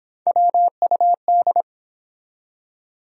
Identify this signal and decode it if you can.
Morse code, heard as a keyed tone, about 25 wpm, 700 Hz. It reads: WUD